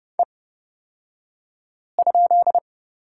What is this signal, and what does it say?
Morse code, heard as a keyed tone, about 30 wpm, spaced as for 5 wpm, 700 Hz. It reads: E?
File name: Lesson 38d - Letter q - Two Characters 30wpm